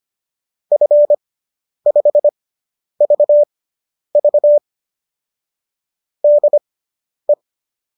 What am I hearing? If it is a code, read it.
Morse code, heard as a keyed tone, about 25 wpm, spaced as for 10 wpm, 590 Hz. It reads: F5VV DE